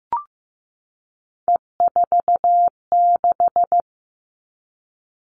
Morse code, heard as a keyed tone, about 15 wpm, 700 Hz. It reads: E46
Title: All Cut Numbers - 3-Digit 15wpm